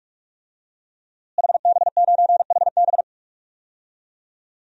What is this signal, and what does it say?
Morse code, heard as a keyed tone, about 44 wpm, 700 Hz. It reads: HB9HB
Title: DXCC-100 Call Signs - 5 Characters - Repeated 2x - 45wpm